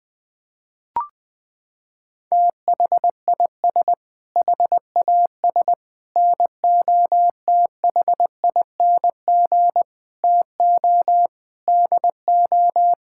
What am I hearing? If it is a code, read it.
Morse code, heard as a keyed tone, about 20 wpm, 700 Hz. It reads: THIS HAS NOTHING TO DO
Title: Sentences with 5-Word Phrases - 20wpm